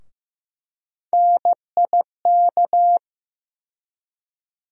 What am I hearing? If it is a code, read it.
Morse code, heard as a keyed tone, about 15 wpm, 700 Hz. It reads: NIK
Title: USA Call Sign Postfixes - 3 Letters 15wpm